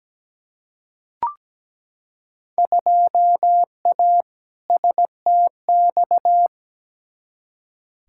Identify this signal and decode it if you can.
Morse code, heard as a keyed tone, about 17 wpm, 700 Hz. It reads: 2A STX
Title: Field Day Exchanges - 17wpm